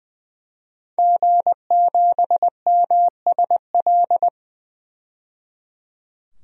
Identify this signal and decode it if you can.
Morse code, heard as a keyed tone, about 20 wpm, 700 Hz. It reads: G7MSL
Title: DXCC-100 Call Signs - 5 Characters - 20wpm